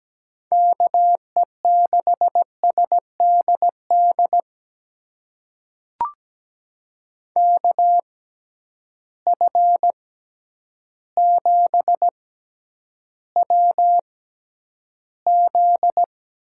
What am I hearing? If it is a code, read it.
Morse code, heard as a keyed tone, about 17 wpm, spaced as for 9 wpm, 700 Hz. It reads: KE6SDD KF7WZ
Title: USA Call Signs - 6 Characters - 6x Character Spacing - 17wpm